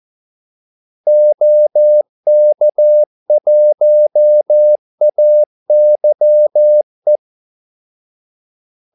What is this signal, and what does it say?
Morse code, heard as a keyed tone, about 14 wpm, 600 Hz. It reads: OK1AYE
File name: Callsigns-14wpm